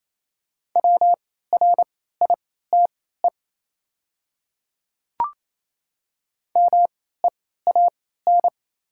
Morse code, heard as a keyed tone, about 28 wpm, spaced as for 16 wpm, 700 Hz. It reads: WRITE MEAN
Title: 3x Character Spacing - Top 200 Words - 28wpm